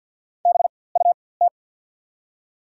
Morse code, heard as a keyed tone, about 54 wpm, spaced as for 24 wpm, 690 Hz. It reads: BUT
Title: 4x Character Spacing - Top 100 Words - 50wpm